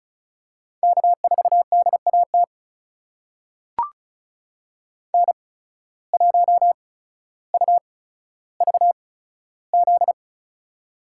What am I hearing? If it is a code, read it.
Morse code, heard as a keyed tone, about 36 wpm, spaced as for 15 wpm, 700 Hz. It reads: K4DAT N1UVZ
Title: USA Call Signs - 5 Characters - 8x Character Spacing - 35wpm